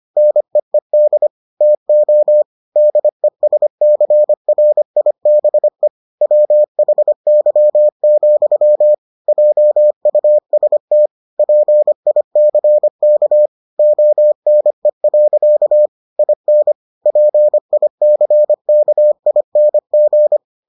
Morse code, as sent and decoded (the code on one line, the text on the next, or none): -. . -.. / - --- / -.. ... -.-. .-. .. -... / .-- .... -.-- --..-- / .--- ..- ... - / .--. .. -.-. -.- / --- -. .-.-.- / .. -. / .--. .. -.-. -.- .. -. --.
NEED TO DESCRIBE WHY, JUST PICK ONE. IN PICKING